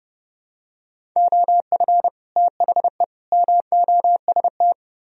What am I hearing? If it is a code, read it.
Morse code, heard as a keyed tone, about 30 wpm, 700 Hz. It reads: OF THE MOST